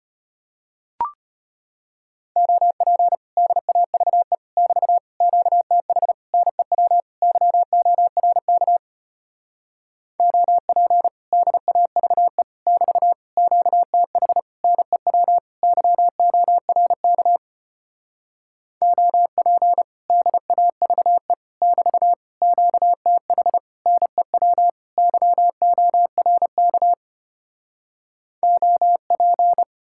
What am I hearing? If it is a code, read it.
Morse code, heard as a keyed tone, about 33 wpm, 700 Hz. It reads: OP DAVE <BT> QTH NEW YORK OP DAVE <BT> QTH NEW YORK OP DAVE <BT> QTH NEW YORK OP